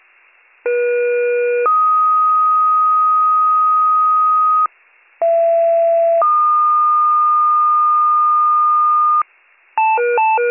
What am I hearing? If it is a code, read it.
Two-tone page, heard as a one-second tone, then a three-second tone, about 480 then 1250 Hz, about 670 then 1220 Hz. It is heard from 0.6 to 9.2 s.